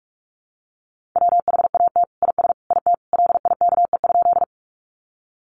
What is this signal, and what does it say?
Morse code, heard as a keyed tone, about 44 wpm, 700 Hz. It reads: WHAT IS IT LIKE?
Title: Sentences from Top 100 Words 45wpm